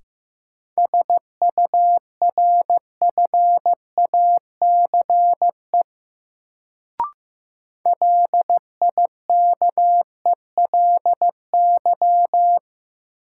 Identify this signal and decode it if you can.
Morse code, heard as a keyed tone, about 15 wpm, 700 Hz. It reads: SURFACE LIKELY
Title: Sentences from Top 600 Words - New Word Drill - 15wpm